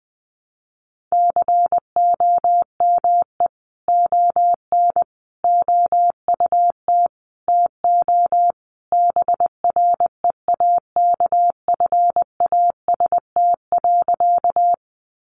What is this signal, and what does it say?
Morse code, heard as a keyed tone, about 20 wpm, 700 Hz. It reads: COME ON OUT TO BREAKFAST.
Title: Sentences from Top 1500 Words 20wpm